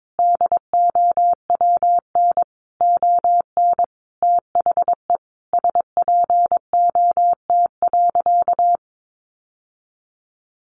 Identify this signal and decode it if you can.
Morse code, heard as a keyed tone, about 22 wpm, 700 Hz. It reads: DOWN ON THE SPOT.
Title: Sentences from Top 1200 Words 22wpm